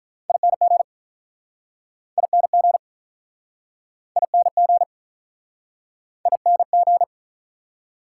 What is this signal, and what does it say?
Morse code, heard as a keyed tone, about 48 wpm, 690 Hz. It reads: ING ING ING ING